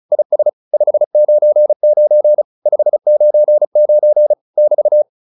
Morse code, heard as a keyed tone, about 36 wpm, 600 Hz. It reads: IS 599 599 <BT>
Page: advanced-practice-files